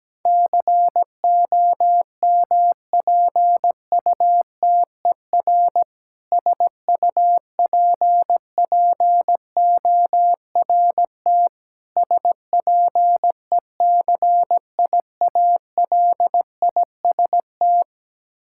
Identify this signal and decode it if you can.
Morse code, heard as a keyed tone, about 17 wpm, 700 Hz. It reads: COMPUTER SUPPORT SPECIALIST
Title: Practice QSO - Exchange 3 - More Info - 17wpm